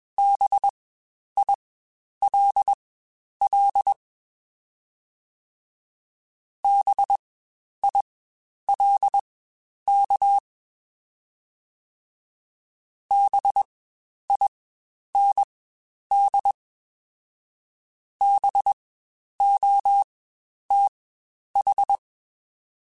Morse code, heard as a keyed tone, about 21 wpm, spaced as for 10 wpm, 800 Hz. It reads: BILL BILK BIND BOTH
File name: Spaced Alphas / Spaced Alphas mp3s / Bs sp @20